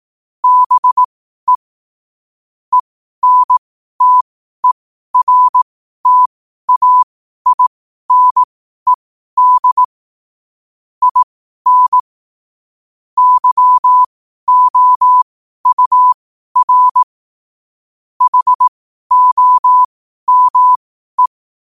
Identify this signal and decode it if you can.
Morse code, heard as a keyed tone, about 18 wpm, spaced as for 12 wpm, 1000 Hz. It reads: BE ENTERTAINED IN YOUR HOME
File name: qotdfiles / 12wpm